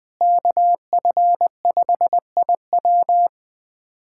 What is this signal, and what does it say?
Morse code, heard as a keyed tone, about 19 wpm, 700 Hz. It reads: KF5IW
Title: USA Callsign DE Callsign - 5 Characters - 20wpm